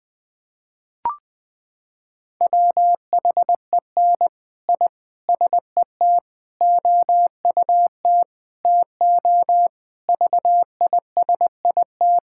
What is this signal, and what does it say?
Morse code, heard as a keyed tone, about 21 wpm, 700 Hz. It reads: WHEN I SET OUT TO VISIT
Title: Sentences from Top 1000 Words - Review Part 6 - 20wpm